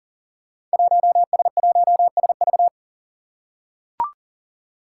Morse code, heard as a keyed tone, about 39 wpm, 700 Hz. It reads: 1S1SV
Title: Five Challenging Letter-Number Combinations 40wpm